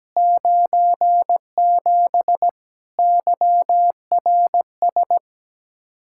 Morse code, heard as a keyed tone, about 17 wpm, 700 Hz. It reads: 97 YRS
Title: Practice QSO - Exchange 3 - More Info - 17wpm